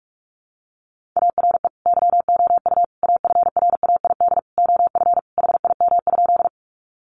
Morse code, heard as a keyed tone, about 44 wpm, 700 Hz. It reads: ARE YOU AFRAID OF HIM?
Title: Sentences from Top 1500 Words 45wpm